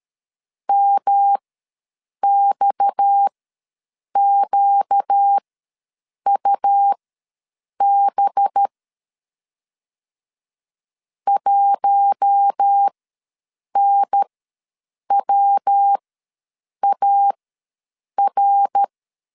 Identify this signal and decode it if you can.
Morse code, heard as a keyed tone, about 13 wpm, spaced as for 7 wpm, 800 Hz. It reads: MXQUB 1NWAR
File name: LEZIONI / 6 PKX0 / 6 PKX ws 12wpm 3106